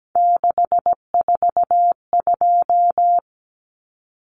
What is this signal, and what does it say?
Morse code, heard as a keyed tone, about 17 wpm, 700 Hz. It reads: 642
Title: Numbers - 3-Digit 17wpm